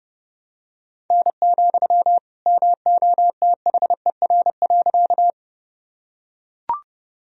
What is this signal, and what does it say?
Morse code, heard as a keyed tone, about 32 wpm, 700 Hz. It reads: N, MOTHER.
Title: Sentences from Top 200 Words and Abbreviations 30wpm